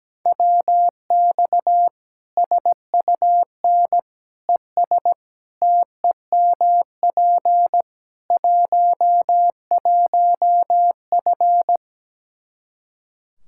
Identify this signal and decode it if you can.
Morse code, heard as a keyed tone, about 17 wpm, 700 Hz. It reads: WX SUN ES TEMP 11F
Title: Practice QSO - Exchange 2 - Info - 17wpm